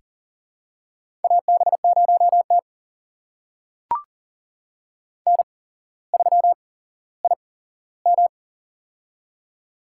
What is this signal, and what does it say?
Morse code, heard as a keyed tone, about 41 wpm, spaced as for 17 wpm, 700 Hz. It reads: AB0T N3IM
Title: USA Call Signs - 4 Characters - 8x Character Spacing - 40wpm